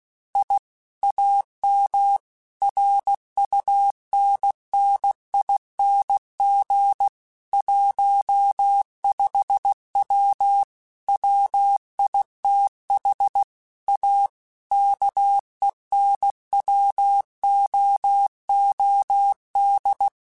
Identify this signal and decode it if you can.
Morse code, heard as a keyed tone, about 16 wpm, 800 Hz. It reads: I AM RUNNING 15W WITH A KENWOOD
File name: QSOs / QSOs-mp3s / WD0BGP 1@15